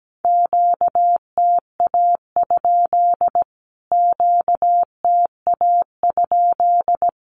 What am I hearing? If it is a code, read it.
Morse code, heard as a keyed tone, about 17 wpm, 700 Hz. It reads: QTA? QTA?
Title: Q-Codes Repeated 3x 17wpm